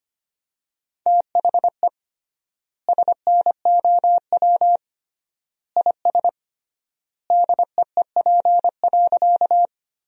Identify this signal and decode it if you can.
Morse code, heard as a keyed tone, about 25 wpm, 700 Hz. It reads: THE SNOW IS DEEP.
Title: 3x Word-Spacing - Sentences from Top 500 Words - 25wpm